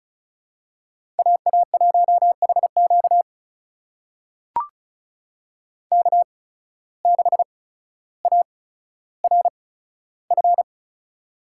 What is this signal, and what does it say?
Morse code, heard as a keyed tone, about 36 wpm, spaced as for 15 wpm, 700 Hz. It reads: AA1HQ K6ARF
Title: USA Call Signs - 5 Characters - 8x Character Spacing - 35wpm